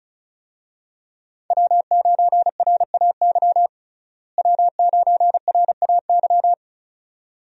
Morse code, heard as a keyed tone, about 34 wpm, 700 Hz. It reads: W9RAY W9RAY
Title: USA Call Signs - 5 Characters - Repeated 2x - 35wpm